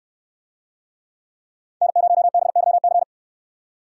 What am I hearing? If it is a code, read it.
Morse code, heard as a keyed tone, about 62 wpm, 700 Hz. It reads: N0BYC